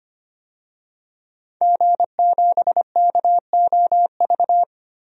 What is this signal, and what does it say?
Morse code, heard as a keyed tone, about 25 wpm, 700 Hz. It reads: G7KOV
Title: DXCC-100 Call Signs - 5 Characters - 25wpm